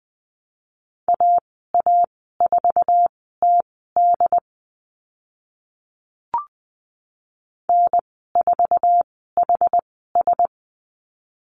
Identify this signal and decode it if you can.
Morse code, heard as a keyed tone, about 20 wpm, spaced as for 14 wpm, 700 Hz. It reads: AA4TD N4HS
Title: 082 Lesson 14g - Number 2 - Call Signs 20wpm